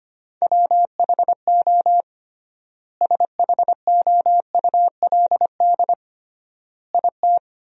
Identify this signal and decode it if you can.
Morse code, heard as a keyed tone, about 25 wpm, 700 Hz. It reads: WHO SHOULD IT
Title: 3x Word-Spacing - Sentences from Top 300 Words - 25wpm